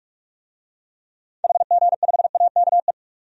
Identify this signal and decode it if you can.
Morse code, heard as a keyed tone, about 46 wpm, 700 Hz. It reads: HG5AKE
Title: DXCC-100 Call Signs - 6 Characters - 45wpm